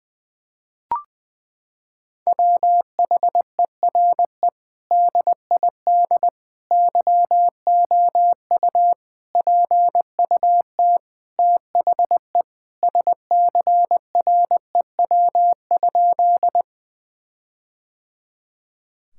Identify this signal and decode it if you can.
Morse code, heard as a keyed tone, about 21 wpm, 700 Hz. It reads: WHERE DID YOU PUT THE SCREW?
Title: Sentences from Top 1600 Words 20wpm